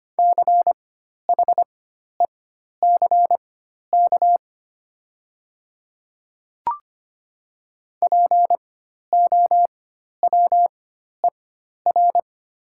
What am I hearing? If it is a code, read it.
Morse code, heard as a keyed tone, about 25 wpm, spaced as for 12 wpm, 700 Hz. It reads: CHECK POWER
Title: 4x Character Spacing - Top 500 Words - 25wpm